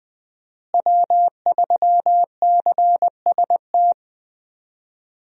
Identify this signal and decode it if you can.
Morse code, heard as a keyed tone, about 21 wpm, 700 Hz. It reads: W3CST
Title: USA Call Signs - 5 Characters - Repeated 2x - 20wpm